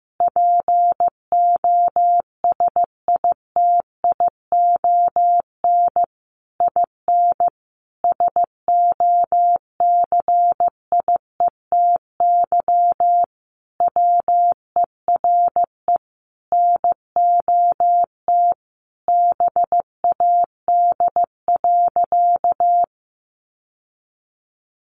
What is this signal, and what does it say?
Morse code, heard as a keyed tone, about 15 wpm, 700 Hz. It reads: POSITION IN SOCIETY WERE NOT BAD.
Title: Sentences from Top 900 Words 15wpm